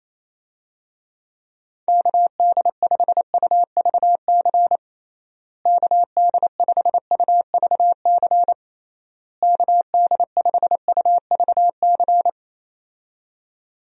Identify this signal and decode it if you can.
Morse code, heard as a keyed tone, about 27 wpm, 700 Hz. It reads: KD5UVC KD5UVC KD5UVC
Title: USA Call Signs - 6 Characters - Repeated 3x - 28wpm